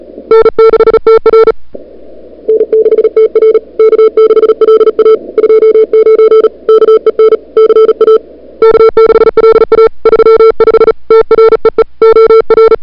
Morse code, heard as a keyed tone, about 37 wpm, 440 Hz. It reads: N6TR N6TR K6LA 29 KEN CA K6LA 35 TREE OR
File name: CWSprint / February 2007 CW Sprint / 01-2007 CW Sprint Hour1